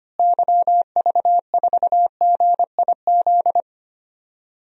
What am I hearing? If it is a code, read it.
Morse code, heard as a keyed tone, about 25 wpm, 700 Hz. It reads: YV4GIZ
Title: DXCC-100 Call Signs - 6 Characters - Repeated 2x - 25wpm